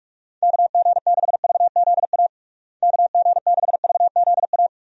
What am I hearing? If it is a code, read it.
Morse code, heard as a keyed tone, about 43 wpm, 700 Hz. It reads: KG6VZA KG6VZA
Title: USA Call Signs - 6 Characters - Repeated 3x - 45wpm